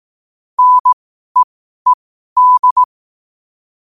Morse code, heard as a keyed tone, about 18 wpm, spaced as for 12 wpm, 1000 Hz. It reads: NEED
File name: qotdfiles / 12wpm